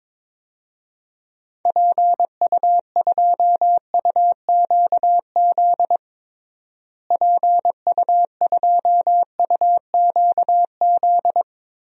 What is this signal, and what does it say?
Morse code, heard as a keyed tone, about 22 wpm, 700 Hz. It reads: PU2UQZ PU2UQZ
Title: DXCC-100 Call Signs - 6 Characters - Repeated 3x - 22wpm